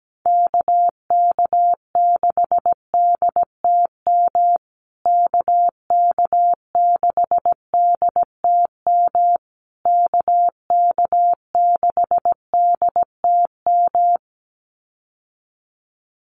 Morse code, heard as a keyed tone, about 17 wpm, 700 Hz. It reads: KK6DTM KK6DTM KK6DTM
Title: USA Call Signs Repeated 3x 17wpm